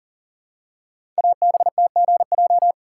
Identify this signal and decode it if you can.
Morse code, heard as a keyed tone, about 40 wpm, 690 Hz. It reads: ABTGJ
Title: Code Groups - 5 Letters - 40wpm